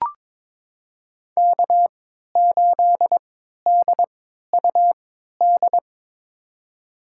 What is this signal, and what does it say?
Morse code, heard as a keyed tone, about 22 wpm, spaced as for 13 wpm, 700 Hz. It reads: K8DUD
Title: USA Call Signs - 3x Character Spacing - 22wpm